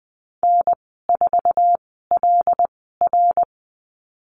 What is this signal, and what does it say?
Morse code, heard as a keyed tone, about 20 wpm, spaced as for 14 wpm, 700 Hz. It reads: N4LR
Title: 122 Lesson 20g - Number 3 - Call Signs 20wpm